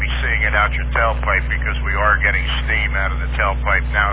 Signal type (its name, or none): Emergency Alert System (SAME) tones